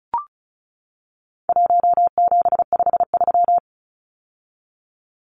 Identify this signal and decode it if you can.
Morse code, heard as a keyed tone, about 34 wpm, 700 Hz. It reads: 1753
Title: Numbers - 4-Digit 35wpm